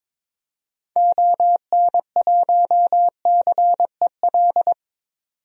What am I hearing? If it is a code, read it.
Morse code, heard as a keyed tone, about 22 wpm, 700 Hz. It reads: ON1CEL
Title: DXCC-100 Call Signs - 6 Characters - 22wpm